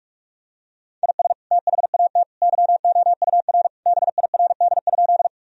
Morse code, heard as a keyed tone, about 41 wpm, 700 Hz. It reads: IS THAT YOUR BIRD?
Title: Sentences from Top 1000 Words - Review Part 2 - 45wpm